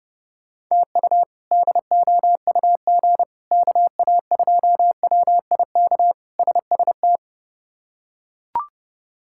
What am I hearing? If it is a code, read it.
Morse code, heard as a keyed tone, about 32 wpm, 700 Hz. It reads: TU DOUG KA2WIK SST
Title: K1USN SST QSO Practice - 30wpm